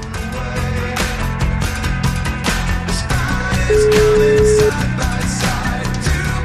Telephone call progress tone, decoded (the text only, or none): ringback tone